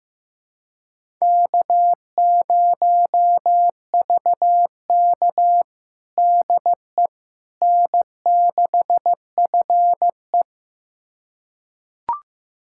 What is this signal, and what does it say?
Morse code, heard as a keyed tone, about 15 wpm, 700 Hz. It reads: K0VK DE N6FE
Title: USA Callsign DE Callsign - 4 Characters - 15wpm